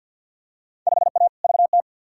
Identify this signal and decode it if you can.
Morse code, heard as a keyed tone, about 46 wpm, 700 Hz. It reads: 5A VT
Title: Field Day Exchanges Encore - 50wpm